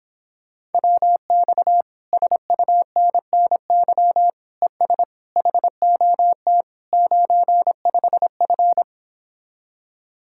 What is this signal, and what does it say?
Morse code, heard as a keyed tone, about 26 wpm, 700 Hz. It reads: WX SUNNY ES HOT 95F